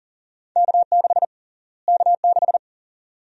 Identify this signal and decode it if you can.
Morse code, heard as a keyed tone, about 40 wpm, 700 Hz. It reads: K6 K6
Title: USA Call Sign Prefixes - Repeated 2x - 40wpm